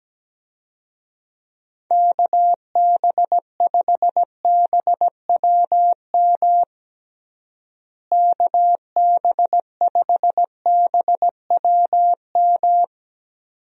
Morse code, heard as a keyed tone, about 17 wpm, 700 Hz. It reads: KB5BWM KB5BWM
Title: USA Call Signs - 6 Characters - Repeated 2x - 17wpm